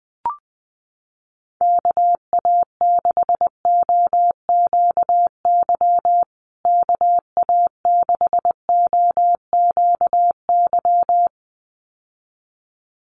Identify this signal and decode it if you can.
Morse code, heard as a keyed tone, about 20 wpm, 700 Hz. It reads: KA6OQY KA6OQY
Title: USA Call Signs Repeated 2x 20wpm